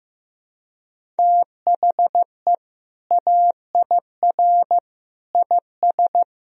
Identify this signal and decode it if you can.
Morse code, heard as a keyed tone, about 16 wpm, 700 Hz. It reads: THE AIR IS
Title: Sentences from Top 1000 Words - Review Part 1 - 15wpm